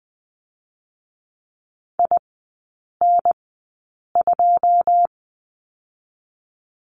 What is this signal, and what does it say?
Morse code, heard as a keyed tone, about 20 wpm, spaced as for 8 wpm, 700 Hz. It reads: IN2
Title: 087 Lesson 15e - Number 5 - Three Characters 20wpm